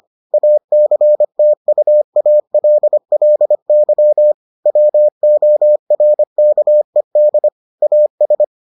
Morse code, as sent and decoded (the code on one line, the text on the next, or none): .- -.-. - ..- .- .-.. .-.. -.-- / .-- --- .-. -.- . -.. / .- ...
ACTUALLY WORKED AS